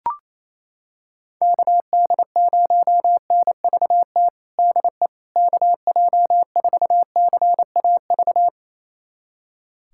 Morse code, heard as a keyed tone, about 28 wpm, 700 Hz. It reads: KD0NVT DE KJ4CAV
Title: USA Callsign DE Callsign 28wpm